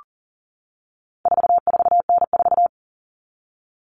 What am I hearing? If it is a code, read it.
Morse code, heard as a keyed tone, about 40 wpm, 700 Hz. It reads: 44N4
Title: All Cut Numbers - 4-Digit 40wpm